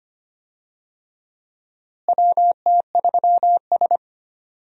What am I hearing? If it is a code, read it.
Morse code, heard as a keyed tone, about 25 wpm, 700 Hz. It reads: WT3S
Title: USA Call Signs - 4 Characters - Repeated 3x - 25wpm